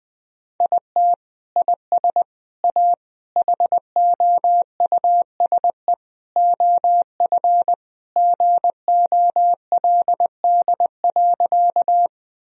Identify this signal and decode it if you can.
Morse code, heard as a keyed tone, about 21 wpm, 700 Hz. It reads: IT IS A HOUSE OF GOLD.
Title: Sentences from Top 1000 Words - Review Part 3 - 20wpm